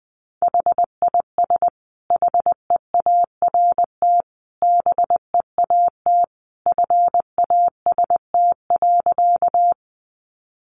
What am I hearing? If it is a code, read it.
Morse code, heard as a keyed tone, about 19 wpm, 700 Hz. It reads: HIS HEART BEAT FAST.
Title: Sentences from Top 1300 Words 20wpm